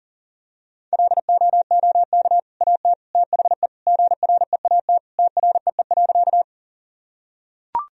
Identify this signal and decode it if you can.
Morse code, heard as a keyed tone, about 40 wpm, 700 Hz. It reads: LOOK AT THE GREAT TREE.
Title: Sentences from Top 1000 Words - Review Part 1 - 40wpm